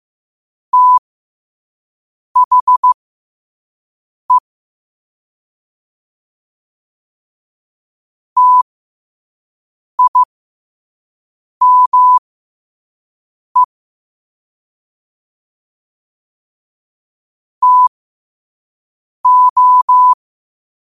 Morse code, heard as a keyed tone, about 15 wpm, spaced as for 5 wpm, 1000 Hz. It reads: THE TIME TO
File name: qotdfiles / 5wpm